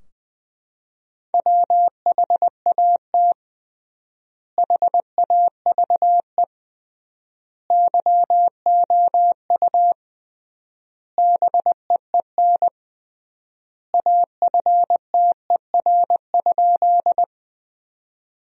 Morse code, heard as a keyed tone, about 20 wpm, 700 Hz. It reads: WHAT HAVE YOU BEEN AFTER?
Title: 3x Word-Spacing - Sentences from Top 200 Words - 20wpm